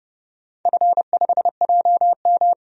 Morse code, heard as a keyed tone, about 30 wpm, 700 Hz. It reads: F5JM
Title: DXCC-100 Call Signs - 4 Characters - 30wpm